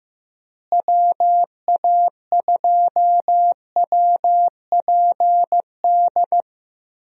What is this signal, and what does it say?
Morse code, heard as a keyed tone, about 15 wpm, 700 Hz. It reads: WA2WPD
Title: USA Call Signs - 6 Characters - Repeated 2x - 15wpm